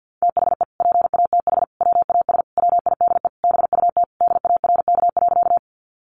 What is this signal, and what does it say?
Morse code, heard as a keyed tone, about 50 wpm, 700 Hz. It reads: THE PATH WAS WIDE BUT DARK.
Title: Sentences from Top 1400 Words 50wpm